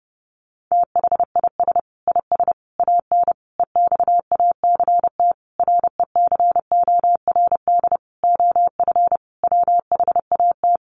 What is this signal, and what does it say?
Morse code, heard as a keyed tone, about 30 wpm, 700 Hz. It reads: THIS IS AN EXACT RECORD OF WHAT
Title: Sentences from Top 1300 Words 30wpm